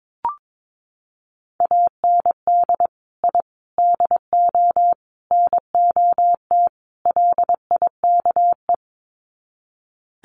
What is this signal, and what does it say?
Morse code, heard as a keyed tone, about 22 wpm, 700 Hz. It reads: AND I DO NOT LIKE
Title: Sets of 5 Words Encore2 22wpm